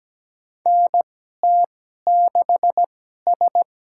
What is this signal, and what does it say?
Morse code, heard as a keyed tone, about 17 wpm, spaced as for 12 wpm, 700 Hz. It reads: NT6S
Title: USA Call Signs - 4 Characters - 2x Character Spacing - 17wpm